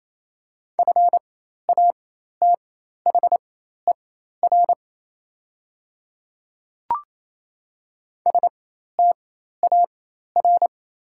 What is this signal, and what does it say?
Morse code, heard as a keyed tone, about 28 wpm, spaced as for 13 wpm, 700 Hz. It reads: FATHER STAR
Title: 4x Character Spacing - Top 500 Words - 28wpm